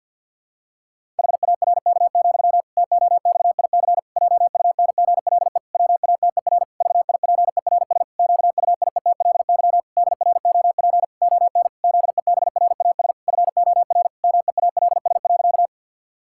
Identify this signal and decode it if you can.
Morse code, heard as a keyed tone, about 47 wpm, 700 Hz. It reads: HARK, TOXIC JUNGLE WATER VIPERS QUIETLY DROP ON ZEBRAS FOR MEALS.
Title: Pangram Challenge v2 - 50wpm